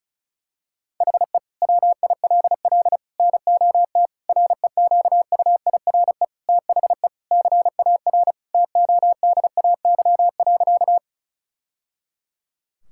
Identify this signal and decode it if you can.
Morse code, heard as a keyed tone, about 34 wpm, 700 Hz. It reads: HE WILL NOT REQUIRE THE CAR TODAY.
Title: Sentences from Top 1000 Words - Review Part 6 - 35wpm